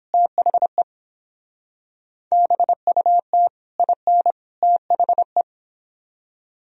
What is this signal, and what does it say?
Morse code, heard as a keyed tone, about 29 wpm, 700 Hz. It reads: THE BUT IN THE